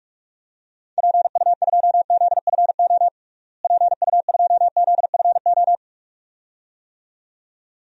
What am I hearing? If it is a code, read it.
Morse code, heard as a keyed tone, about 46 wpm, 700 Hz. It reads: PU2ZFO PU2ZFO
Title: DXCC-100 Call Signs - 6 Characters - Repeated 2x - 45wpm